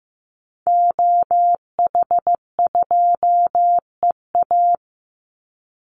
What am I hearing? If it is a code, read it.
Morse code, heard as a keyed tone, about 15 wpm, 700 Hz. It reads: OH2EA
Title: Call Signs Repeated 2x - 15wpm